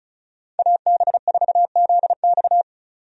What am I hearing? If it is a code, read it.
Morse code, heard as a keyed tone, about 36 wpm, 700 Hz. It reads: AB4ZX
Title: USA Call Signs - 5 Characters - 35wpm